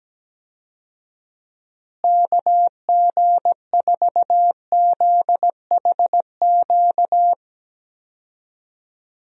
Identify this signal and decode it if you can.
Morse code, heard as a keyed tone, about 17 wpm, 700 Hz. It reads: KG4ZHQ
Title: USA Call Signs - 6 Characters - Repeated 3x - 17wpm